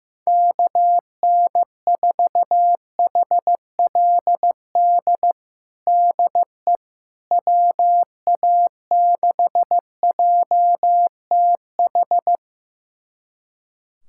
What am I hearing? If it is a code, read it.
Morse code, heard as a keyed tone, about 15 wpm, 700 Hz. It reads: KN4HLD DE WA6JTH
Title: USA Callsign DE Callsign 15wpm